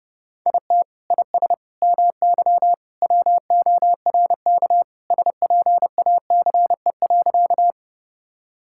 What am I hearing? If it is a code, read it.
Morse code, heard as a keyed tone, about 30 wpm, 700 Hz. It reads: IT IS MY WORK SPACE.
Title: Sentences from Top 1000 Words - Review Part 5 - 30wpm